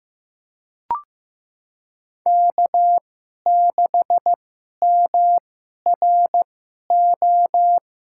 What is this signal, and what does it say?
Morse code, heard as a keyed tone, about 15 wpm, spaced as for 11 wpm, 700 Hz. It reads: K6MRO
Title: USA Call Signs - 5 Characters - 2x Character Spacing - 15wpm